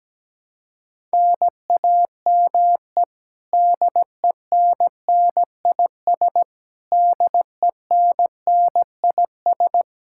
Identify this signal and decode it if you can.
Morse code, heard as a keyed tone, about 17 wpm, 700 Hz. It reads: NAME DENNIS DENNIS
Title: Practice QSO - Exchange 1 - Beginning - 17wpm